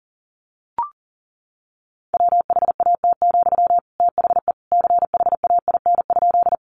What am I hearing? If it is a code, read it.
Morse code, heard as a keyed tone, about 40 wpm, 700 Hz. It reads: WHAT, THE CHAIN?
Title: Sentences from Top 1500 Words 40wpm